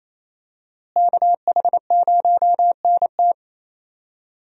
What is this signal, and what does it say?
Morse code, heard as a keyed tone, about 28 wpm, 700 Hz. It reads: KH0NT
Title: USA Call Signs - 5 Characters - Repeated 2x - 28wpm